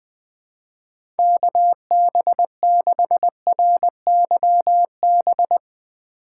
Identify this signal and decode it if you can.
Morse code, heard as a keyed tone, about 20 wpm, 700 Hz. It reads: KB6RYB
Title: USA Call Signs - 2x Character Spacing - 20wpm